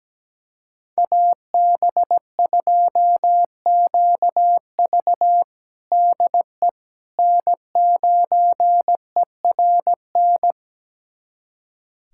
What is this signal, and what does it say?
Morse code, heard as a keyed tone, about 17 wpm, 700 Hz. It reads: AB2QV DE N9ERN
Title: USA Callsign DE Callsign - 5 Characters - 17wpm